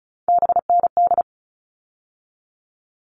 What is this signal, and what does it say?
Morse code, heard as a keyed tone, about 34 wpm, 700 Hz. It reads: BND
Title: All Cut Numbers - 3-Digit 35wpm